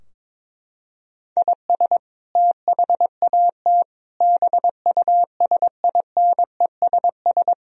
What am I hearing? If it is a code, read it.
Morse code, heard as a keyed tone, about 21 wpm, 700 Hz. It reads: IS THAT BUSINESS
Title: Sentences from Top 1000 Words - Review Part 3 - 22wpm